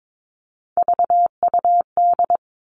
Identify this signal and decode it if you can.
Morse code, heard as a keyed tone, about 22 wpm, 700 Hz. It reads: VUD
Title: Three Challenging Letter-Number Combinations 22wpm